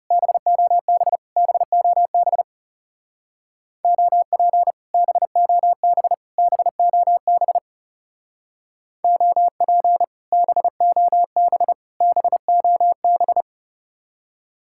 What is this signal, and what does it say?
Morse code, heard as a keyed tone, about 36 wpm, 700 Hz. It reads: BOB BOB OP BOB BOB OP BOB BOB
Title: Speed-Racing Common QSO Phrases - 30wpm